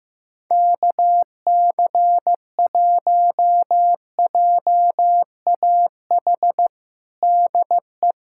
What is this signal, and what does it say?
Morse code, heard as a keyed tone, about 15 wpm, 700 Hz. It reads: KC1JAH DE
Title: USA Callsign DE Callsign 15wpm